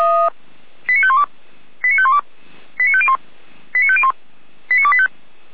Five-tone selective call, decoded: 89512 89512 89501 89501 89285

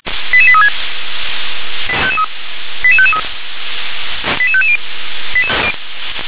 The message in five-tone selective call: 88836 88553 88550

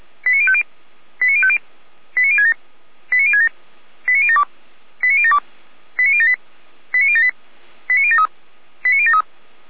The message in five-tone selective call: 89050 89050 89067 89067 89072 89072 89078 89078 89073 89073

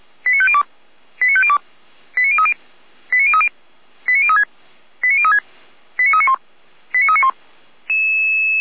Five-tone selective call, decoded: 89502 89502 89030 89030 89036 89036 89391 89391

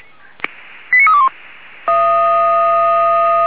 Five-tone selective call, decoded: 89321

